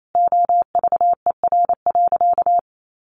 Morse code, heard as a keyed tone, about 28 wpm, 700 Hz. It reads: OVER.
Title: Sentences from Top 1300 Words 28wpm